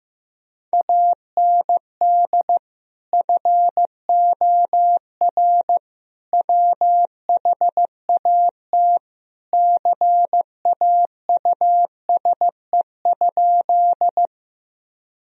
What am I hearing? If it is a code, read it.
Morse code, heard as a keyed tone, about 15 wpm, 700 Hz. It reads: AND FOR WHAT CAUSE?
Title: Sentences from Top 1000 Words - Review Part 1 - 15wpm